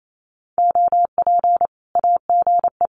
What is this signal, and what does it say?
Morse code, heard as a keyed tone, about 27 wpm, 700 Hz. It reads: OP AGE
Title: Common QSO Phrases 28wpm